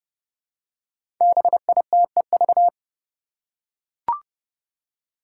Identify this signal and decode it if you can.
Morse code, heard as a keyed tone, about 30 wpm, 700 Hz. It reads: BITEV
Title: Five Challenging Letter-Number Combinations 30wpm